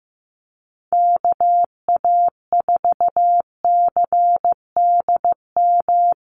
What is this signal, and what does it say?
Morse code, heard as a keyed tone, about 15 wpm, 700 Hz. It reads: KA4CDM
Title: USA Call Signs Repeated 2x 15wpm